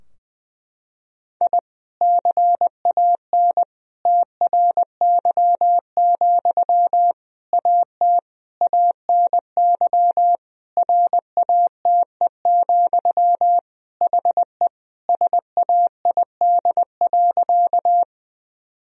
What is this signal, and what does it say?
Morse code, heard as a keyed tone, about 20 wpm, 700 Hz. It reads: I CAN TRY, AT ANY RATE, HE SAID.
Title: Sentences from Top 1000 Words - Review Part 4 - 20wpm